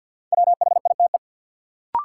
Morse code, heard as a keyed tone, about 53 wpm, 700 Hz. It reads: WHITE
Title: Sentences from Top 300 Words - New Word Drill - 50wpm